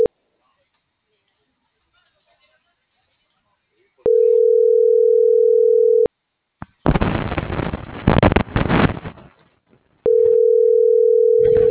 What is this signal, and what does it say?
Ringback tone (North America), 440 + 480 Hz.
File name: CENTRAL TERRESTRE / out erick.sanjuan